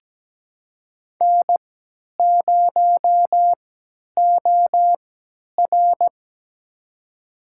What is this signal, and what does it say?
Morse code, heard as a keyed tone, about 17 wpm, spaced as for 10 wpm, 700 Hz. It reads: N0OR
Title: USA Call Signs - 3x Character Spacing - 17wpm